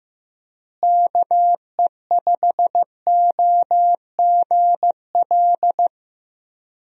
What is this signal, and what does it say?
Morse code, heard as a keyed tone, about 16 wpm, 700 Hz. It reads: KE5OGL
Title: USA Call Signs - 6 Characters - Repeated 2x - 15wpm